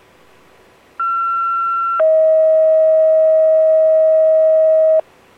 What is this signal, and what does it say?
Two-tone page, heard as a one-second tone, then a three-second tone, about 1360 then 640 Hz. It is heard from 1.0 to 5.0 s.